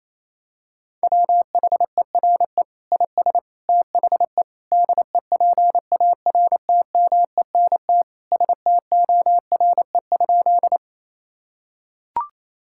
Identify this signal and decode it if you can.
Morse code, heard as a keyed tone, about 28 wpm, 700 Hz. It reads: WHERE IS THE DEPARTMENT STORE?
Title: Sentences from Top 1000 Words - Review Part 5 - 28wpm